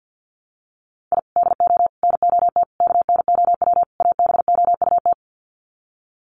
Morse code, heard as a keyed tone, about 50 wpm, 700 Hz. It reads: I DO NOT KNOW ABOUT
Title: Sets of 5 Words 50wpm